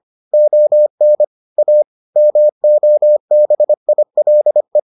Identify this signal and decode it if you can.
Morse code, heard as a keyed tone, about 25 wpm, 600 Hz. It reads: ON A MOBILE